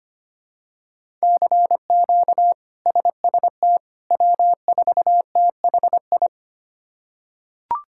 Morse code, heard as a keyed tone, about 25 wpm, 700 Hz. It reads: CQ SST W4THI
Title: K1USN SST QSO Practice - 25wpm